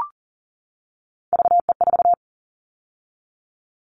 Morse code, heard as a keyed tone, about 40 wpm, 700 Hz. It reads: VE4
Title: Callsign Prefixes for Top 50 DXCC - 40wpm